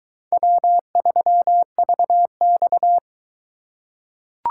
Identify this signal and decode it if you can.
Morse code, heard as a keyed tone, about 21 wpm, 700 Hz. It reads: W3VX